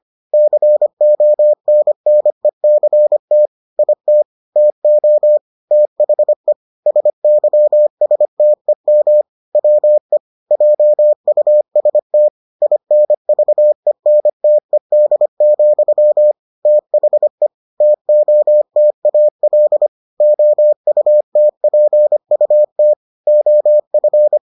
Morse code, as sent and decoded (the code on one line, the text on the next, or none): -.-. --- -. -. -.-. - / .. - / - --- / - .... / ... -.-- ... - . -- / .-- . / .--- ..- ... - / .. -. ...- . -. - . -.. --..-- / - .... / - --- - .- .-.. / --- ..- - .--. ..- - / --- ..-.
CONNECT IT TO THE SYSTEM WE JUST INVENTED, THE TOTAL OUTPUT OF